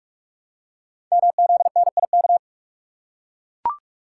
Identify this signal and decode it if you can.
Morse code, heard as a keyed tone, about 44 wpm, 700 Hz. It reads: MZNIK
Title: Code Groups - 5 Letters - 45wpm